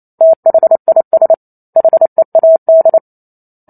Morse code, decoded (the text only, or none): THIS HEAD